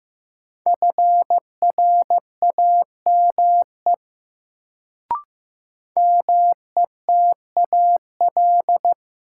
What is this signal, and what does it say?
Morse code, heard as a keyed tone, about 15 wpm, 700 Hz. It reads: FRAME METAL
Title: Sentences from Top 1000 Words - Review Part 6 - New Word Drill - 15wpm